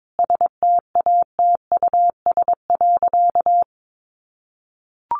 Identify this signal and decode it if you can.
Morse code, heard as a keyed tone, about 22 wpm, 700 Hz. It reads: STATUS.
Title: Sentences from Top 1400 Words 22wpm